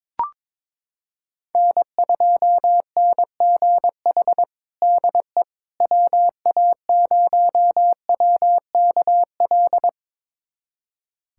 Morse code, heard as a keyed tone, about 22 wpm, 700 Hz. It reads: N2NGH DE WA0WKL
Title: USA Callsign DE Callsign 22wpm